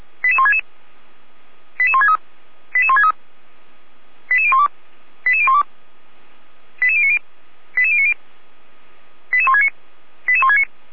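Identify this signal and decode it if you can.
Five-tone selective call (ZVEI), heard as quick runs of five tones, tones of about 70 ms. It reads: 88166 88163 88163 88012 88012 88090 88090 88169 88169
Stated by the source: ZVEI five-tone call received on an SDR.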